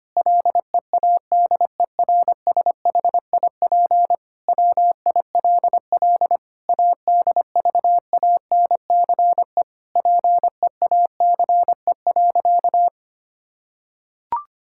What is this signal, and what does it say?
Morse code, heard as a keyed tone, about 25 wpm, 700 Hz. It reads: LEADERSHIP WILL ADVANCE PEACE.
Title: Sentences from Top 1600 Words 25wpm